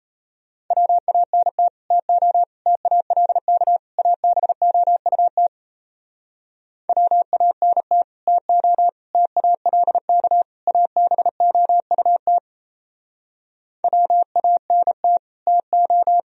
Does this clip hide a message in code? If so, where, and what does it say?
Yes: Morse code from 0.7 to 16.2 s, heard as a keyed tone, about 37 wpm, 700 Hz. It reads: WANT TO TALK ABOUT WANT TO TALK ABOUT WANT TO